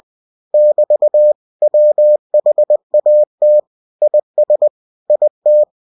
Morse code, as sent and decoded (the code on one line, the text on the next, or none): -...- / .-- .... .- - / .. ... / .. -
<BT> WHAT IS IT